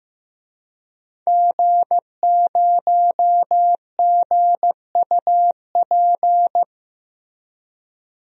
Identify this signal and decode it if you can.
Morse code, heard as a keyed tone, about 15 wpm, 700 Hz. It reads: G0GUP
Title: DXCC-100 Call Signs - 5 Characters - Repeated 3x - 15wpm